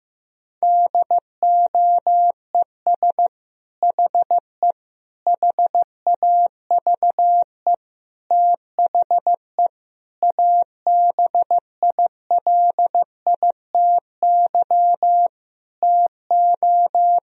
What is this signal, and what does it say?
Morse code, heard as a keyed tone, about 15 wpm, 700 Hz. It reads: DOES HE HAVE THE ABILITY TO
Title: Sentences from Top 1000 Words - Review Part 3 - 15wpm